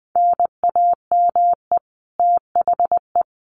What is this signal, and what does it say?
Morse code, heard as a keyed tone, about 20 wpm, 700 Hz. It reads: NAME THE
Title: Sentences from Top 1500 Words 20wpm